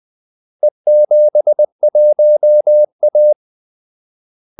Morse code, heard as a keyed tone, about 20 wpm, 600 Hz. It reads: E71A
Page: Kaldesignaler 20 wpm